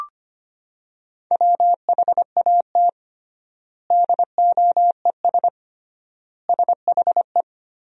Morse code, heard as a keyed tone, about 25 wpm, 700 Hz. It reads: WHAT DOES SHE
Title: 3x Word-Spacing - Sentences from Top 200 Words - 25wpm